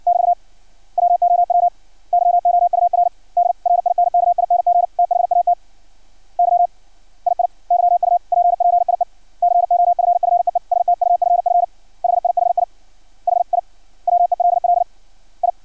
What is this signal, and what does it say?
Morse code, heard as a keyed tone, about 56 wpm, 700 Hz. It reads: <BT> WOW YOUR DRINKING THAT YII YUPPIE COFFEE STUFF HIHI HI WELL I
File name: archived / CRSnet